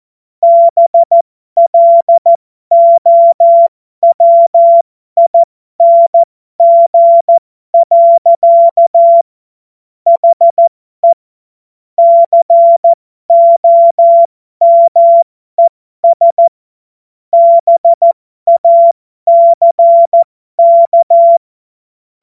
Morse code, heard as a keyed tone, about 14 wpm, spaced as for 12 wpm, 680 Hz. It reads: BLOWING. HE COMES BACK